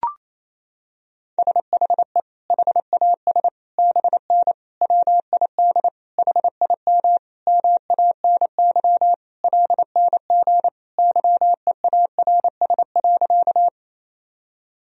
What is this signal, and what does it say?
Morse code, heard as a keyed tone, about 28 wpm, 700 Hz. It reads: SHE HAS BN WID HIM MANY LNG YEARS.
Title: Sentences from Top 200 Words and Abbreviations 28wpm